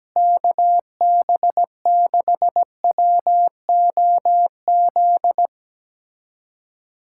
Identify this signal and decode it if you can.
Morse code, heard as a keyed tone, about 17 wpm, 700 Hz. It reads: KB6WOZ
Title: USA Callsign DE Callsign 17wpm